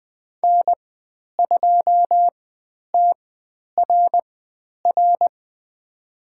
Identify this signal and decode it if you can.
Morse code, heard as a keyed tone, about 20 wpm, spaced as for 10 wpm, 700 Hz. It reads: N2TRR
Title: 095 Lesson 16g - Letter c - Call Signs 20wpm